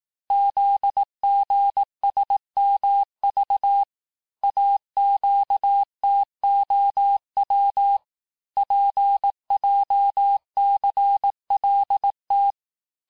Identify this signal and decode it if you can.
Morse code, heard as a keyed tone, about 18 wpm, 800 Hz. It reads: ZGSMV AQTOW PJCLT
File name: cours-du-lundi files / cours-1852-texte-chiffre-vitesse-17MPM